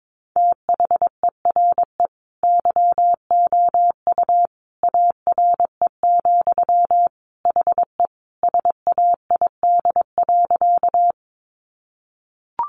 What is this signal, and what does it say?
Morse code, heard as a keyed tone, about 22 wpm, 700 Hz. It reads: THERE YOU ARE, HE SAID.
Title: Sentences from Top 100 Words 22wpm